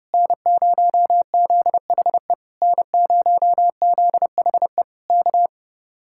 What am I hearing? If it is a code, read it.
Morse code, heard as a keyed tone, about 29 wpm, 700 Hz. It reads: N0ZHE N0ZHE K